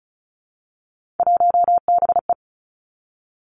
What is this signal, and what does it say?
Morse code, heard as a keyed tone, about 36 wpm, 700 Hz. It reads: 1BE
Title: Three Challenging Letter-Number Combinations 35wpm